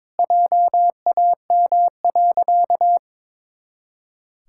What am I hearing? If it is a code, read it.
Morse code, heard as a keyed tone, about 22 wpm, 700 Hz. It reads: JAM.
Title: Pangram Challenge v2 - 22wpm